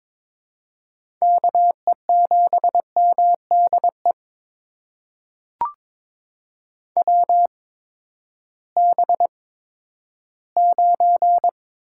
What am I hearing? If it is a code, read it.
Morse code, heard as a keyed tone, about 22 wpm, spaced as for 12 wpm, 700 Hz. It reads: KE7MDE WB9
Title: USA Call Signs - 6 Characters - 8x Character Spacing - 22wpm